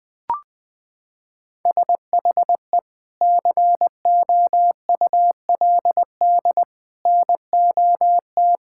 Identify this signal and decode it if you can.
Morse code, heard as a keyed tone, about 20 wpm, 700 Hz. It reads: SHE COULD NOT
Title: Sentences from Top 1600 Words 20wpm